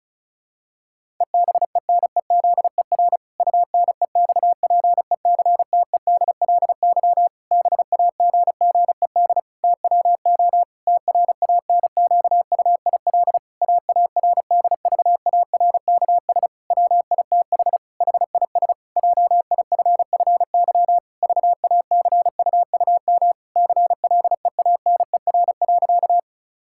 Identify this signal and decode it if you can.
Morse code, heard as a keyed tone, about 36 wpm, 700 Hz. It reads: EBENEZER UNEXPECTEDLY BAGGED TWO TRANQUIL AARDVARKS WITH HIS JIFFY VACUUM CLEANER.
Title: Pangram Challenge v2 - 35wpm